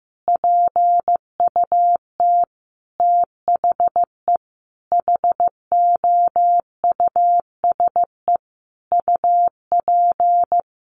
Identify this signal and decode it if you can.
Morse code, heard as a keyed tone, about 15 wpm, 700 Hz. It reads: PUT THE HOUSE UP
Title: Sentences from Top 1400 Words 15wpm